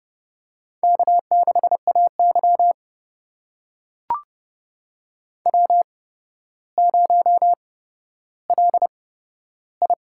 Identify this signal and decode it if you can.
Morse code, heard as a keyed tone, about 28 wpm, spaced as for 13 wpm, 700 Hz. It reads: K6AY W0LI
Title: USA Call Signs - 4 Characters - 8x Character Spacing - 30wpm